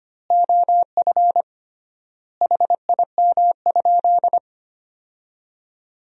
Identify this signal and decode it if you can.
Morse code, heard as a keyed tone, about 25 wpm, 700 Hz. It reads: OF HIM?
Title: 3x Word-Spacing - Sentences from Top 100 Words - 25wpm